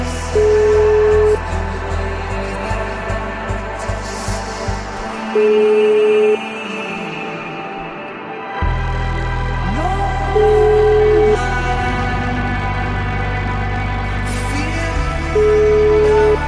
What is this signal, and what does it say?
Ringback tone (Europe, 425 Hz), with noise.